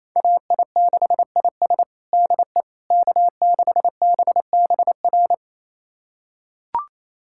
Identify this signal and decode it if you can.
Morse code, heard as a keyed tone, about 28 wpm, 700 Hz. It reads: AI6IS DE K6BBR
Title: USA Callsign DE Callsign - 5 Characters - 28wpm